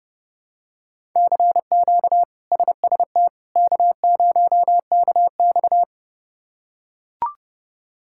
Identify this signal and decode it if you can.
Morse code, heard as a keyed tone, about 30 wpm, 700 Hz. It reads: CQ SST K0KX
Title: K1USN SST QSO Practice - 30wpm